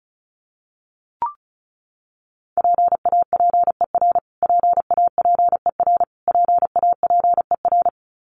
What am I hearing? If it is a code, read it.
Morse code, heard as a keyed tone, about 34 wpm, 700 Hz. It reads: PAPER PAPER PAPER
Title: Top 500 Words Repeated 35wpm